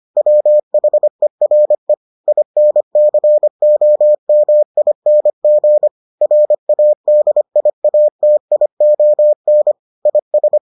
Morse code, heard as a keyed tone, about 25 wpm, 600 Hz. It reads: WHERE INCOMING RADIATION IS